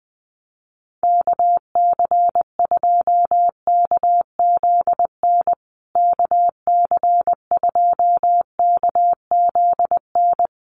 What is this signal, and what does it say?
Morse code, heard as a keyed tone, about 20 wpm, 700 Hz. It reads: KC2KZN KC2KZN
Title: USA Call Signs Repeated 3x 20wpm